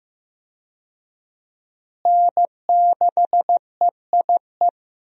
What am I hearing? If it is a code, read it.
Morse code, heard as a keyed tone, about 15 wpm, 700 Hz. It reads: N6EIE
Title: USA Call Signs - 5 Characters - Repeated 3x - 15wpm